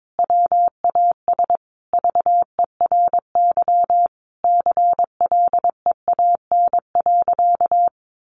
Morse code, heard as a keyed tone, about 22 wpm, 700 Hz. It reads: WAS VERY CLEAN.
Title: Sentences from Top 1100 Words 22wpm